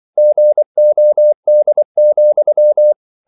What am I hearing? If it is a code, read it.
Morse code, heard as a keyed tone, about 24 wpm, 600 Hz. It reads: GOD,
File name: CW / Practice / EnglishQuotations0324